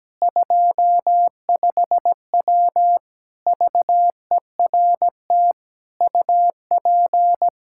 Morse code, heard as a keyed tone, about 17 wpm, 700 Hz. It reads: 25W VERT UP
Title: Practice QSO - Exchange 2 - Info - 17wpm